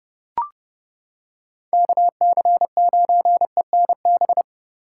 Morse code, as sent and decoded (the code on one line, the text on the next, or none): -.- -.-. ----. -. -...
KC9ENB